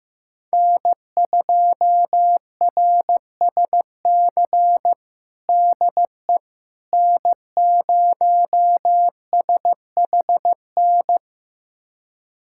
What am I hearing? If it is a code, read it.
Morse code, heard as a keyed tone, about 15 wpm, 700 Hz. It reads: N2RSC DE N0SHN
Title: USA Callsign DE Callsign - 5 Characters - 15wpm